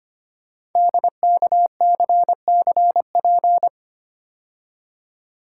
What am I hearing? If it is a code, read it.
Morse code, heard as a keyed tone, about 25 wpm, 700 Hz. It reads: DKCCP
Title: Code Groups - 5 Letters - 25wpm